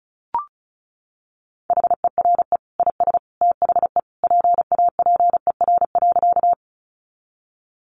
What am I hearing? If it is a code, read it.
Morse code, heard as a keyed tone, about 34 wpm, 700 Hz. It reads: HERE IS THE PAPER.
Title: Sentences from Top 300 Words 35wpm